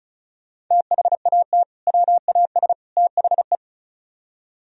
Morse code, heard as a keyed tone, about 44 wpm, 700 Hz. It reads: THAT WAS THE